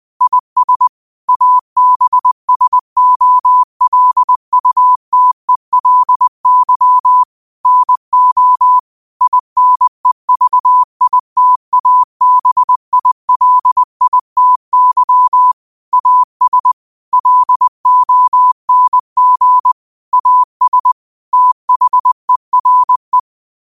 Morse code, heard as a keyed tone, about 20 wpm, 1000 Hz. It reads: IS ABSOLUTELY NO INEVITABILITY AS LONG AS THERE
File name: qotdfiles / 20wpm